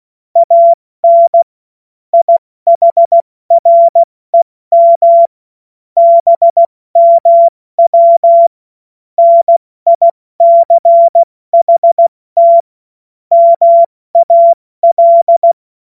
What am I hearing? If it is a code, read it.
Morse code, heard as a keyed tone, about 16 wpm, spaced as for 14 wpm, 680 Hz. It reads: AN IHREM BMW NICHT MAL
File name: Morsekurs / Witze / Witz 02. August 2022 70 F80